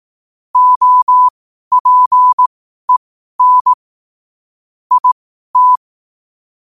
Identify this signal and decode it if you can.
Morse code, heard as a keyed tone, about 18 wpm, spaced as for 12 wpm, 1000 Hz. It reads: OPEN IT
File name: qotdfiles / 12wpm